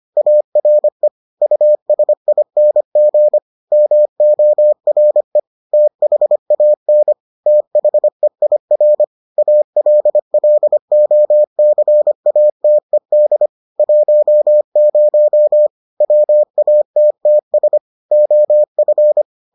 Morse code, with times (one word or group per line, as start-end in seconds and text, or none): 0.2-1.1 ARE
1.4-3.4 USING
3.7-5.4 MORE
5.7-7.1 THAN
7.5-9.0 THEIR
9.4-13.5 ALLOCATED
13.8-15.7 10
16.0-17.8 WATTS
18.1-19.2 OF